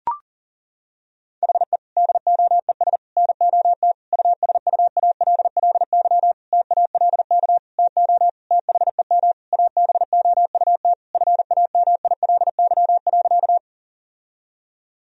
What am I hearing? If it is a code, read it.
Morse code, heard as a keyed tone, about 37 wpm, 700 Hz. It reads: HE DOES NOT USUALLY TALK TO THEM ABOUT FAMILY.
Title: Sentences from Top 1000 Words - Review Part 3 - 40wpm